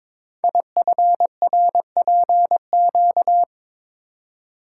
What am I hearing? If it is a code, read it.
Morse code, heard as a keyed tone, about 22 wpm, 700 Hz. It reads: IFRPQ
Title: Code Groups - 5 Letter:Numbers - 22wpm